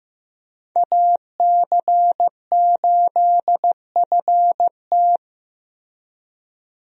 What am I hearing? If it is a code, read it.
Morse code, heard as a keyed tone, about 15 wpm, 700 Hz. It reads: AC8FT
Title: USA Call Signs - 5 Characters - Repeated 3x - 15wpm